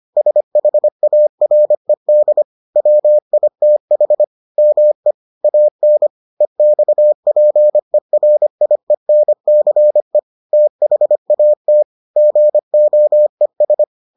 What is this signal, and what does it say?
Morse code, heard as a keyed tone, about 25 wpm, 600 Hz. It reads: SHARED WITH ME AN EXPERIENCE THAT GOES